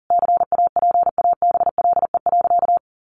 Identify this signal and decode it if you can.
Morse code, heard as a keyed tone, about 40 wpm, 700 Hz. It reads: CAPABLE.
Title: Sentences from Top 1200 Words 40wpm